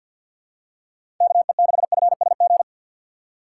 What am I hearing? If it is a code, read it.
Morse code, heard as a keyed tone, about 52 wpm, 700 Hz. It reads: KE6FSG
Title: USA Call Signs - 3x Character Spacing - 50wpm